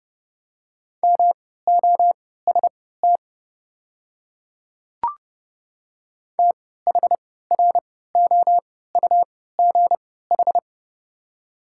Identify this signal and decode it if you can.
Morse code, heard as a keyed tone, about 29 wpm, spaced as for 17 wpm, 690 Hz. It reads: MOST THROUGH
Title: 3x Character Spacing - Top 200 Words - 30wpm